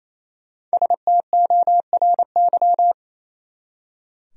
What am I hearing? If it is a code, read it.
Morse code, heard as a keyed tone, about 34 wpm, 700 Hz. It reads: STORY
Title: Speed-Racing Top 300 Words 28wpm